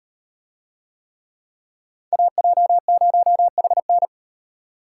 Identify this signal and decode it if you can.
Morse code, heard as a keyed tone, about 36 wpm, 700 Hz. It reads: AJ0HN